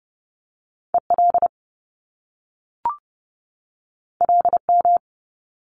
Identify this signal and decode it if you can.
Morse code, heard as a keyed tone, about 30 wpm, 700 Hz. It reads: EL LM
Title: Top Two Letter Combinations in Words 30wpm